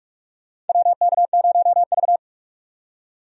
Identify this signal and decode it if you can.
Morse code, heard as a keyed tone, about 46 wpm, 700 Hz. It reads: WK0V
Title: USA Call Signs - 4 Characters - 45wpm